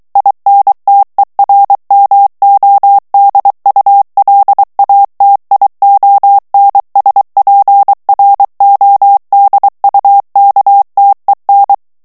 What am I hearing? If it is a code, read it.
Morse code, heard as a keyed tone, about 23 wpm, 800 Hz. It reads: INTERMODULATIONSPRODUKTEN
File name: morse mp3 / ALLEMAND texte clair 1 vitesse 22 mots minute